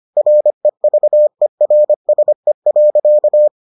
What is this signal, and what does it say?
Morse code, heard as a keyed tone, about 25 wpm, 600 Hz. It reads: REVERSE.